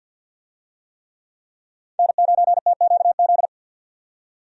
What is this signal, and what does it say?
Morse code, heard as a keyed tone, about 52 wpm, 700 Hz. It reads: N9TQZ
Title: USA Call Signs - 5 Characters - 50wpm